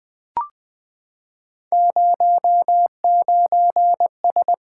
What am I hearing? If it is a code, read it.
Morse code, heard as a keyed tone, about 20 wpm, 700 Hz. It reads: 09S
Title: Three Letter-Number Combinations - 20wpm